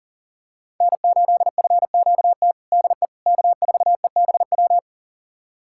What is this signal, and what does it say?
Morse code, heard as a keyed tone, about 40 wpm, 700 Hz. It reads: N8FQT DE K4EBW
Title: USA Callsign DE Callsign - 5 Characters - 40wpm